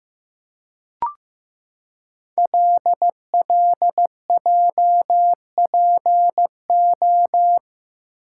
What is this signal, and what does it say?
Morse code, heard as a keyed tone, about 15 wpm, 700 Hz. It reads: LLJPO
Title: Code Groups - 5 Letters - 15wpm